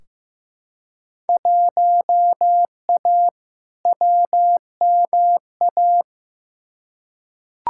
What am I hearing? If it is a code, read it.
Morse code, heard as a keyed tone, about 15 wpm, 700 Hz. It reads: 1A WMA
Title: Field Day Exchanges - 15wpm